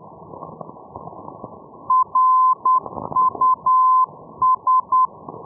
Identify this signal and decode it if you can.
Morse code, heard as a keyed tone, about 10 wpm, 1030 Hz. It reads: RUS
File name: europe / e-314 ndb reus rus-424-1305-day-menorca-rb